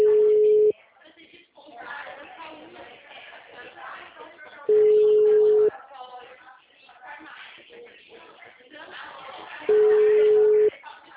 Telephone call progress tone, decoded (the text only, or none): ringback tone